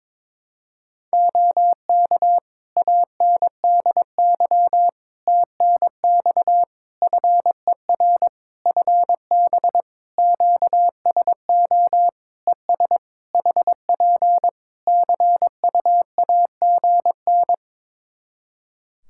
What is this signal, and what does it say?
Morse code, heard as a keyed tone, about 22 wpm, 700 Hz. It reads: OK ANDY TNX FER FB QSO ES HP CUAGN
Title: K1USN SST QSO Practice 22wpm